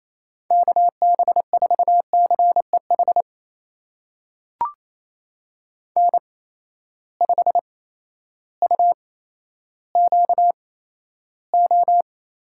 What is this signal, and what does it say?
Morse code, heard as a keyed tone, about 28 wpm, spaced as for 13 wpm, 700 Hz. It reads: KB4CEH N5UQO
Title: USA Call Signs - 8x Character Spacing - 28wpm